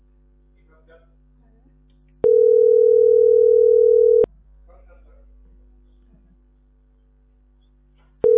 Ringback tone (North America), 440 + 480 Hz.